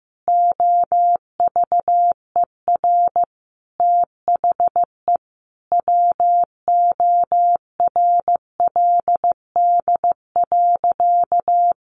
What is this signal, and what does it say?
Morse code, heard as a keyed tone, about 14 wpm, 700 Hz. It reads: OVER THE WORLD.
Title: Additional Easy Phrases 15wpm